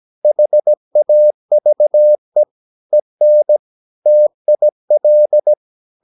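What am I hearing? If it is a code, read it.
Morse code, heard as a keyed tone, about 17 wpm, 600 Hz. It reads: HAVE EN TIL
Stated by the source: Slow (QRS) Morse practice at 17 wpm, from CW med Gnister.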